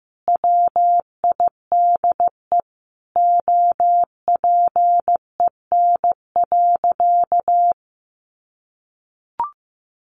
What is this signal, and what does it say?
Morse code, heard as a keyed tone, about 15 wpm, 700 Hz. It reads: WIDE OPEN.
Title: Sentences from Top 1500 Words 15wpm